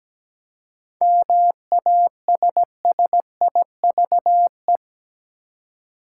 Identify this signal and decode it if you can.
Morse code, heard as a keyed tone, about 17 wpm, 700 Hz. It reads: MASSIVE
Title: Seven Letter Words - 17wpm